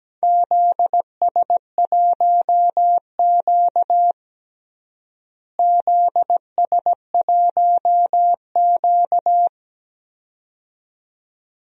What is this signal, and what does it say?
Morse code, heard as a keyed tone, about 17 wpm, 700 Hz. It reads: ZS1Q ZS1Q
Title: DXCC-100 Call Signs - 4 Characters - Repeated 2x - 17wpm